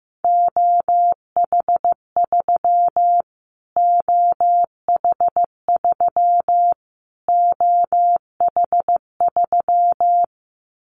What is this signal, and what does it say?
Morse code, heard as a keyed tone, about 15 wpm, 700 Hz. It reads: OH3 OH3 OH3
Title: Callsign Prefixes for Top 25 DXCC - Repeated 3x - 15wpm